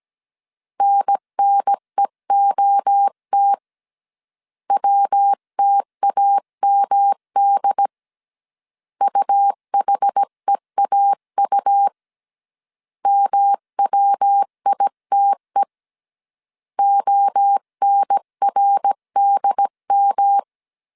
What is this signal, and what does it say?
Morse code, heard as a keyed tone, about 17 wpm, 800 Hz. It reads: NNEOT WTAMD UHEAU MWITE ONRDM